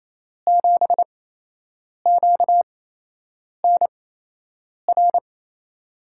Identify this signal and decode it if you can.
Morse code, heard as a keyed tone, about 28 wpm, spaced as for 8 wpm, 700 Hz. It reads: 7QNR
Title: USA Call Signs - 8x Character Spacing - 28wpm